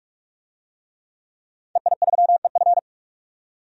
Morse code, heard as a keyed tone, about 41 wpm, 700 Hz. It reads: EI3EF
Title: DXCC-100 Call Signs - 5 Characters - 45wpm